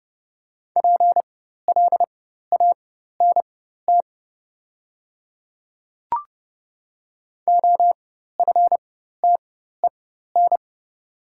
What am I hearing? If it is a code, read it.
Morse code, heard as a keyed tone, about 30 wpm, spaced as for 14 wpm, 700 Hz. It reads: PLANT OFTEN
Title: 4x Character Spacing - Top 300 Words - 30wpm